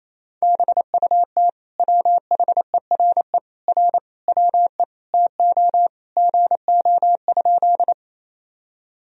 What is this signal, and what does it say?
Morse code, heard as a keyed tone, about 28 wpm, 700 Hz. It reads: BUT WHERE R WE TO GO?
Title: Sentences from Top 200 Words and Abbreviations 28wpm